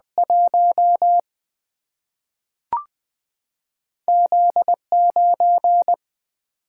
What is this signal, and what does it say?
Morse code, heard as a keyed tone, about 20 wpm, 700 Hz. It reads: 1 Z9
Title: Single-Letter Single-Number 20wpm